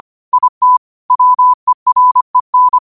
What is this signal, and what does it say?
Morse code, heard as a keyed tone, about 25 wpm, 1000 Hz. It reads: IT WEREN
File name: qotdfiles / 25wpm